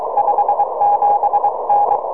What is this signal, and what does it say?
Morse code, heard as a keyed tone, about 23 wpm, 820 Hz, with noise. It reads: 57N
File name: tropo / sp5bin 23cm